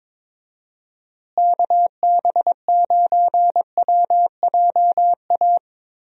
Morse code, heard as a keyed tone, about 22 wpm, 700 Hz. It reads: KB9WJA
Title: USA Call Signs - 6 Characters - Repeated 3x - 22wpm